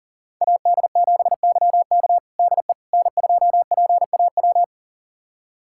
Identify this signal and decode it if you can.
Morse code, heard as a keyed tone, about 37 wpm, 700 Hz. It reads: AD7YK DE N2PAW
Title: USA Callsign DE Callsign - 5 Characters - 40wpm